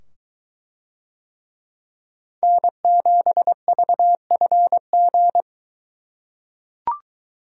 Morse code, heard as a keyed tone, about 20 wpm, 700 Hz. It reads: N7VFG